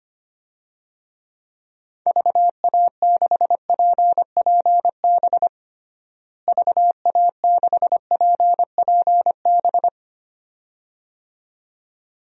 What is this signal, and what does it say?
Morse code, heard as a keyed tone, about 25 wpm, 700 Hz. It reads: VA6PPB VA6PPB
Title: DXCC-100 Call Signs - 6 Characters - Repeated 2x - 25wpm